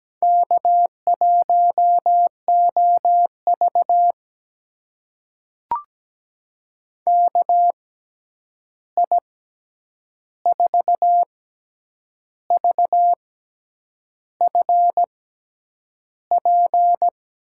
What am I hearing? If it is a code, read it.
Morse code, heard as a keyed tone, about 17 wpm, spaced as for 8 wpm, 700 Hz. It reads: K1OV KI4VFP
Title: USA Call Signs - 6x Character Spacing - 17wpm